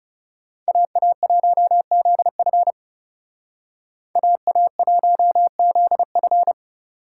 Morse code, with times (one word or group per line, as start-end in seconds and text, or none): 0.7-2.7 AA1ZF
4.2-6.5 AA1ZF